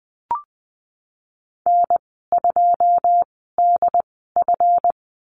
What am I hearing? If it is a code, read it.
Morse code, heard as a keyed tone, about 20 wpm, spaced as for 15 wpm, 700 Hz. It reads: N2DF
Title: 186 Lesson 30g - Letter b - Call Signs 20wpm